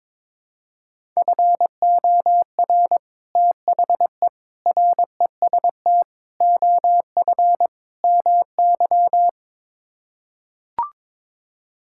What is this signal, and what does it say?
Morse code, heard as a keyed tone, about 22 wpm, 700 Hz. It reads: FOR THE REST OF MY
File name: practice / 22wpm / Build A Phrase - 5-Word Phrases - 22wpm